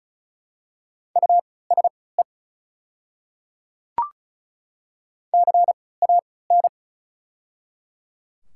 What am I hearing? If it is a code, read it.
Morse code, heard as a keyed tone, about 36 wpm, spaced as for 20 wpm, 700 Hz. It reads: USE CAN
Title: 3x Character Spacing - Top 200 Words - 35wpm